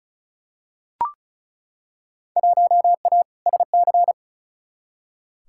Morse code, heard as a keyed tone, about 36 wpm, 700 Hz. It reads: 1A SC
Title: Field Day Exchanges - 35wpm